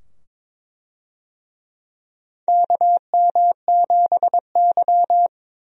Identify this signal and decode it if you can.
Morse code, heard as a keyed tone, about 22 wpm, 700 Hz. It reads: KM7Y
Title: USA Call Signs - 4 Characters - 22wpm